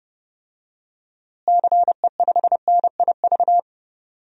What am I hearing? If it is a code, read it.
Morse code, heard as a keyed tone, about 30 wpm, 700 Hz. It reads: CE5NIV
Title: DXCC-100 Call Signs - 6 Characters - Repeated 3x - 30wpm